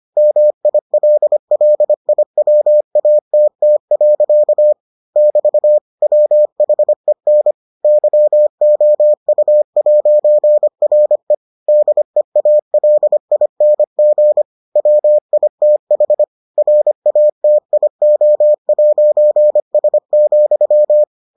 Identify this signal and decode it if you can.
Morse code, heard as a keyed tone, about 25 wpm, 600 Hz. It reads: MILLIWATT. <BT> WHEN YOU'RE DEALING WITH RATIO'S,